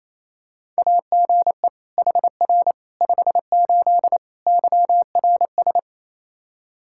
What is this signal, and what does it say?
Morse code, heard as a keyed tone, about 27 wpm, 700 Hz. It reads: AGE HR 58 YRS
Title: Practice QSO - Exchange 3 - More Info - 28wpm